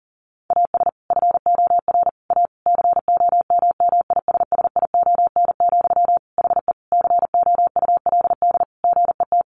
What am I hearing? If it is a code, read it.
Morse code, heard as a keyed tone, about 37 wpm, 700 Hz. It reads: AS FOR A COMMISSION, HE COULD GET
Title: Sentences from Top 1400 Words 40wpm